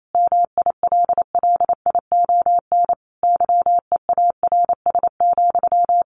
Morse code, heard as a keyed tone, about 28 wpm, 700 Hz. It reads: MILLION YEARS,
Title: A Sound Of Thunder 28wpm